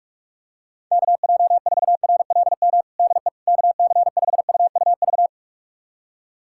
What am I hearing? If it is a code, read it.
Morse code, heard as a keyed tone, about 44 wpm, 700 Hz. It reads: KJ4RLM DE KC5UUV
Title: USA Callsign DE Callsign - 6 Characters - 45wpm